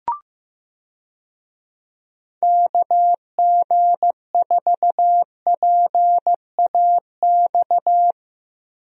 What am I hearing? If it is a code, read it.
Morse code, heard as a keyed tone, about 15 wpm, 700 Hz. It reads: KG4PAX
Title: USA Call Signs - 6 Characters - Repeated 2x - 15wpm